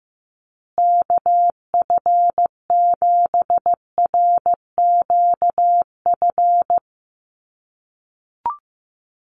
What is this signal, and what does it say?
Morse code, heard as a keyed tone, about 15 wpm, 700 Hz. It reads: KF7RQF
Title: USA Call Signs Repeated 2x 15wpm